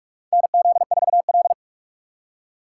Morse code, heard as a keyed tone, about 47 wpm, 700 Hz. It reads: NZ4L
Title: USA Call Signs - 6x Character Spacing - 45wpm